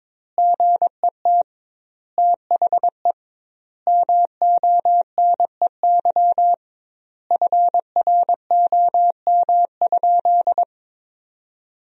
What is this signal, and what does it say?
Morse code, heard as a keyed tone, about 21 wpm, 700 Hz. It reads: GET THE MONEY FROM?
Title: 2x Word-Spacing - Sentences from Top 400 Words - 22wpm